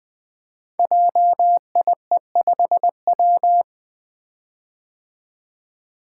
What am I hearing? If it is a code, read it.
Morse code, heard as a keyed tone, about 20 wpm, 700 Hz. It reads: JIE5W
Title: Code Groups - 5 Letter:Numbers - 20wpm